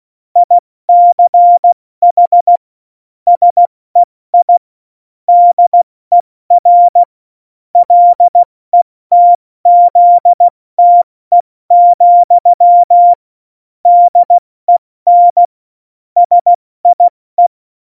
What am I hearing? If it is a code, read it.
Morse code, heard as a keyed tone, about 16 wpm, 700 Hz. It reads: ICH SEI DER LETZTE, DEN SIE
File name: Morsekurs / Witze / Witz 16. Juli 2019, 70 F80